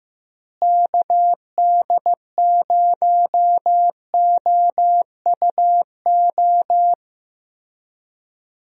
Morse code, heard as a keyed tone, about 15 wpm, 700 Hz. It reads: KD0OUO
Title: USA Call Signs - 6 Characters - Repeated 3x - 15wpm